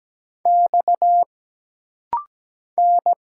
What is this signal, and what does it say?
Morse code, heard as a keyed tone, about 17 wpm, 700 Hz. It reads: X N
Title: Single Letters - Rapid-Fire 17wpm